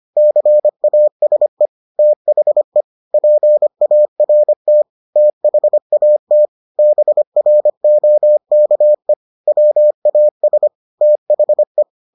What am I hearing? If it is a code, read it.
Morse code, heard as a keyed tone, about 25 wpm, 600 Hz. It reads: CASE THE PART THAT BROKE WAS THE